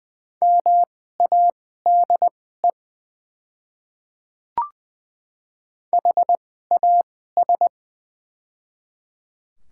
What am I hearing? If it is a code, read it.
Morse code, heard as a keyed tone, about 20 wpm, spaced as for 14 wpm, 700 Hz. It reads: MADE HAS
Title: 2x Character Spacing - Top 300 Words - 20wpm